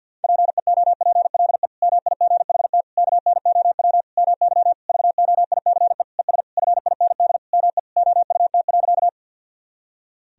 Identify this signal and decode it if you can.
Morse code, heard as a keyed tone, about 46 wpm, 700 Hz. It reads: PEOPLE MIGHT KNOW MY VOICE ES FIND ME OUT.
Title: Sentences from Top 500 Words and Abbreviations 50wpm